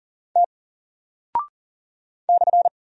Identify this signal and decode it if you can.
Morse code, heard as a keyed tone, about 40 wpm, 700 Hz. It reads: T /
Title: Single Character (Letter:Number:Punctuation:Prosign) Alt 40wpm